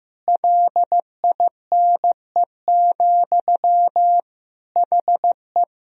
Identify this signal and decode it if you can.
Morse code, heard as a keyed tone, about 15 wpm, 700 Hz. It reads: LINE, HE
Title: Sentences from Top 1000 Words - Review Part 1 - 15wpm